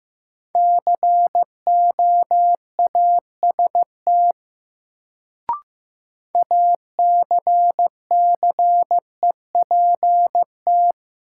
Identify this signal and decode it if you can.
Morse code, heard as a keyed tone, about 15 wpm, 700 Hz. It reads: COAST ACCEPT
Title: Sentences from Top 1300 Words - New Word Drill Alt - 15wpm